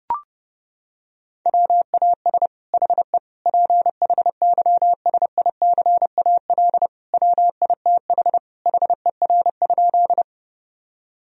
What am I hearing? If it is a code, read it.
Morse code, heard as a keyed tone, about 30 wpm, 700 Hz. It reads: WAS HE PHYSICAL WITH HER?
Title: Sentences from Top 1000 Words - Review Part 4 - 30wpm